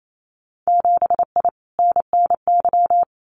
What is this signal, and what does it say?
Morse code, heard as a keyed tone, about 28 wpm, 700 Hz. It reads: 7I NNY
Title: Winter Field Day Exchanges Encore 28wpm